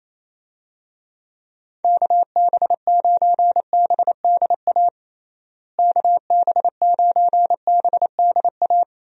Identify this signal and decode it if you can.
Morse code, heard as a keyed tone, about 28 wpm, 700 Hz. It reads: KB9BDA KB9BDA
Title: USA Call Signs - 6 Characters - Repeated 2x - 28wpm